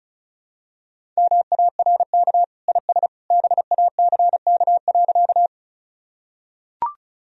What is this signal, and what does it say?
Morse code, heard as a keyed tone, about 36 wpm, 700 Hz. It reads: MARK IS BACK.
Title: Sentences from Top 1000 Words - Review Part 1 - 35wpm